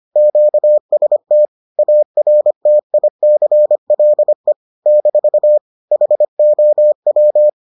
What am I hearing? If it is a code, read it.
Morse code, heard as a keyed tone, about 25 wpm, 600 Hz. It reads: QST ARTICLE - HOW